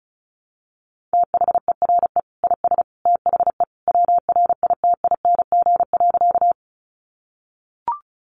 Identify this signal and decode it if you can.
Morse code, heard as a keyed tone, about 34 wpm, 700 Hz. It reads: THERE IS THE WRITING.
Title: Sentences from Top 800 Words 35wpm